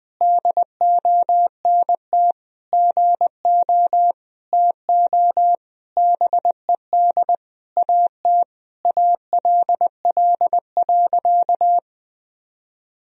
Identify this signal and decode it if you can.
Morse code, heard as a keyed tone, about 21 wpm, 700 Hz. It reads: DONT GO TO BED AT ALL.
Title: Sentences from Top 1000 Words - Review Part 3 - 20wpm